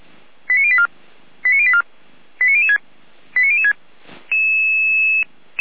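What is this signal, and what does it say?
Five-tone selective call (ZVEI), heard as quick runs of five tones, tones of about 70 ms. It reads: 89084 89084 89006 89006
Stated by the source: ZVEI five-tone selective call 89340, received by SDR.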